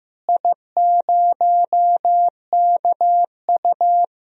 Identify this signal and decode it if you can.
Morse code, heard as a keyed tone, about 16 wpm, 700 Hz. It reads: I0KU
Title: DXCC-100 Call Signs - 4 Characters - Repeated 3x - 15wpm